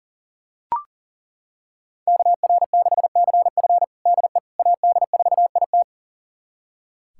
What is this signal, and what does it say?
Morse code, heard as a keyed tone, about 40 wpm, 700 Hz. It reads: KR6CF DE AD4IT
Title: USA Callsign DE Callsign - 5 Characters - 40wpm